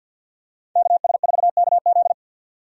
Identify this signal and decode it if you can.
Morse code, heard as a keyed tone, about 52 wpm, 700 Hz. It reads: KS4KZ
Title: USA Call Signs - 5 Characters - 50wpm